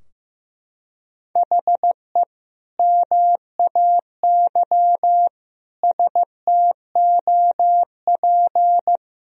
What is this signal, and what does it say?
Morse code, heard as a keyed tone, about 15 wpm, 700 Hz. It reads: HE MAY STOP
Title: Sentences from Top 1000 Words - Review Part 2 - 15wpm